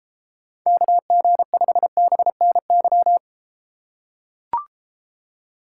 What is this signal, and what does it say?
Morse code, heard as a keyed tone, about 32 wpm, 700 Hz. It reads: KG5BNY